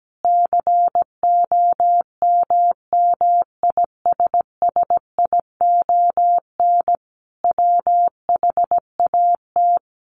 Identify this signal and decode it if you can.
Morse code, heard as a keyed tone, about 17 wpm, 700 Hz. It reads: COMMISSION WHAT
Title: Sentences from Top 1400 Words 17wpm